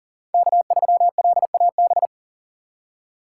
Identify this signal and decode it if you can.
Morse code, heard as a keyed tone, about 40 wpm, 700 Hz. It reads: K3LAB
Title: USA Call Signs - 6x Character Spacing - 40wpm